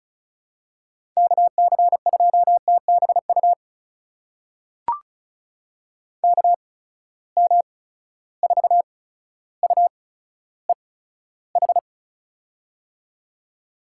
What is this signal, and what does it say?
Morse code, heard as a keyed tone, about 34 wpm, spaced as for 15 wpm, 700 Hz. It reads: KC2TBU KM4UEH
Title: USA Call Signs - 6 Characters - 8x Character Spacing - 35wpm